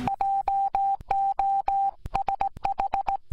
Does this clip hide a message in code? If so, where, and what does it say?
Yes: Morse code from 0.1 to 3.2 s, heard as a keyed tone, about 18 wpm, 790 Hz. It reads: JOSH